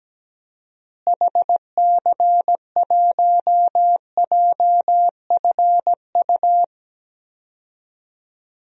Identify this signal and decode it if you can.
Morse code, heard as a keyed tone, about 17 wpm, 700 Hz. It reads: HC1JFU
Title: DXCC-100 Call Signs - 6 Characters - Repeated 3x - 17wpm